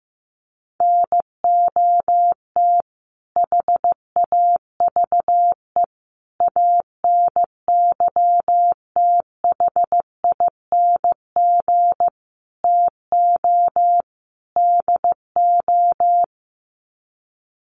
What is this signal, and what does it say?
Morse code, heard as a keyed tone, about 14 wpm, 700 Hz. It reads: NOT HAVE ANYTHING TO DO
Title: Sets of 5 Words Encore 15wpm